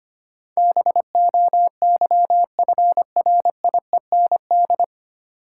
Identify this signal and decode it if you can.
Morse code, heard as a keyed tone, about 25 wpm, 700 Hz. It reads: BOYFRIEND
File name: practice / 25wpm / 9 Letter Words - First 3 Letters Given - 25wpm